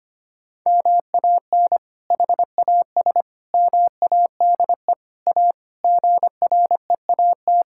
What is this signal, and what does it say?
Morse code, heard as a keyed tone, about 25 wpm, 700 Hz. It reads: MAN HAS MADE A GREAT
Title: Sentences from Top 1000 Words - Review Part 6 - 25wpm